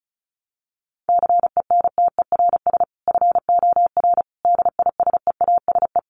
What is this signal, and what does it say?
Morse code, heard as a keyed tone, about 34 wpm, 700 Hz. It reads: CENTERS FOR DISEASE
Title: Sets of 5 Words Encore2 35wpm